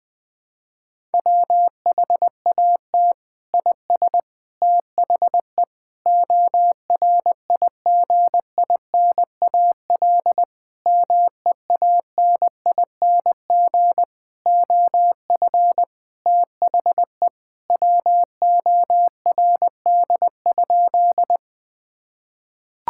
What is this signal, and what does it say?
Morse code, heard as a keyed tone, about 20 wpm, 700 Hz. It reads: WHAT IS THE ORIGINAL MEANING OF THE WORD?
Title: Sentences from Top 1000 Words - Review Part 6 - 20wpm